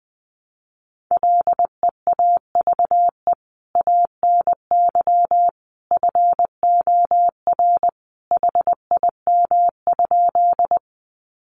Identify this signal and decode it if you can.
Morse code, heard as a keyed tone, about 20 wpm, 700 Hz. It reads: LEAVE ANY FOR HIM?
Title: Sentences from Top 400 Words 20wpm